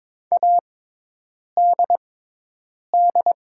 Morse code, heard as a keyed tone, about 22 wpm, spaced as for 8 wpm, 700 Hz. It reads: ADD
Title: 6x Character Spacing - Top 500 Words - 22wpm